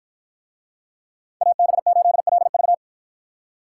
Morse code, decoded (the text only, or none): AB8LV